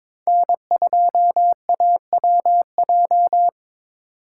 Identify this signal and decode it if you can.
Morse code, heard as a keyed tone, about 26 wpm, 700 Hz. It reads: N2AWJ
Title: Speed Racing - USA 1x3 Call Signs 22wpm